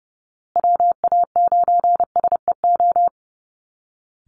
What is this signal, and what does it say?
Morse code, heard as a keyed tone, about 30 wpm, 700 Hz. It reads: WA9SEO
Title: Call Signs Repeated 2x - 30wpm